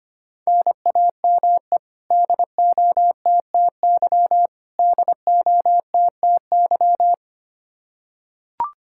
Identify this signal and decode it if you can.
Morse code, heard as a keyed tone, about 25 wpm, 700 Hz. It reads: NAME DOTTY DOTTY
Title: Practice QSO - Exchange 1 - Beginning - 25wpm